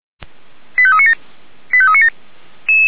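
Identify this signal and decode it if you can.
Five-tone selective call (ZVEI), heard as quick runs of five tones, tones of about 70 ms. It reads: 85397 85397
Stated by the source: Received on SDR, ZVEI five-tone call.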